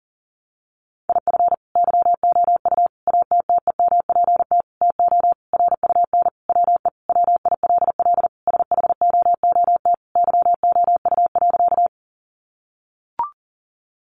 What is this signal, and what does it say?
Morse code, heard as a keyed tone, about 40 wpm, 700 Hz. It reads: IF YOU ATTEMPT TO RUN WE WILL SHOOT YOU.
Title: Sentences from Top 1200 Words 40wpm